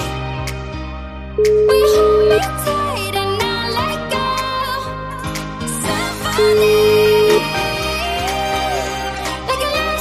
Ringback tone (Europe, 425 Hz), with noise.